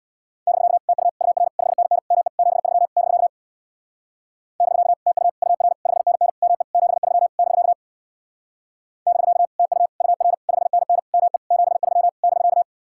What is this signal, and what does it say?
Morse code, heard as a keyed tone, about 61 wpm, 700 Hz. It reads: <BK> TU UR 5NN ME 73 <BK> <BK> TU UR 5NN ME 73 <BK> <BK> TU UR 5NN ME 73 <BK>
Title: Speed-Racing Common QSO Phrases Encore - 50wpm